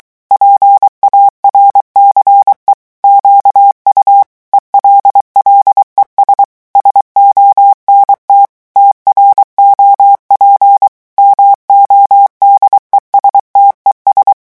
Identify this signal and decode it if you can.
Morse code, heard as a keyed tone, about 23 wpm, 800 Hz. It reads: PARCE QU ELLES SONT TROP MODESTES
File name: morse mp3 / revision9 citations humoristiques vitesse 22 mots minute